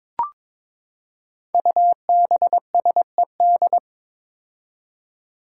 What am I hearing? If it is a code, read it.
Morse code, heard as a keyed tone, about 22 wpm, 700 Hz. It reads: UBSED
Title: Five Challenging Letter-Number Combinations 22wpm